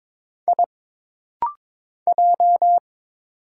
Morse code, heard as a keyed tone, about 21 wpm, 700 Hz. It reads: I J
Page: This rapid-fire timing is geared to push your ICR (Instant Character Recognition) skills.